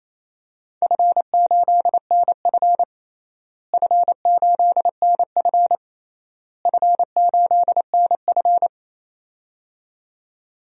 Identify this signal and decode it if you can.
Morse code, heard as a keyed tone, about 28 wpm, 700 Hz. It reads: F8NF F8NF F8NF
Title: DXCC-100 Call Signs - 4 Characters - Repeated 3x - 28wpm